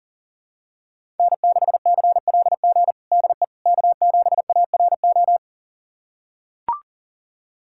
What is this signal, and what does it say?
Morse code, heard as a keyed tone, about 40 wpm, 700 Hz. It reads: N6CLG DE K7ARO
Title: USA Callsign DE Callsign - 5 Characters - 40wpm